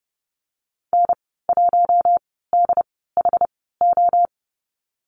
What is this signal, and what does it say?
Morse code, heard as a keyed tone, about 31 wpm, spaced as for 17 wpm, 700 Hz. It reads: N1DHO
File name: course / 30:10wpm / 116 Lesson 19g - Letter w - Call Signs 30wpm